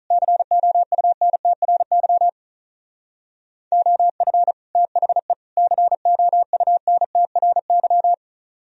Morse code, heard as a keyed tone, about 43 wpm, 700 Hz. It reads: COUNTRY OF THE COUNTRY
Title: Speed-Racing - Sets of 3 Words 35wpm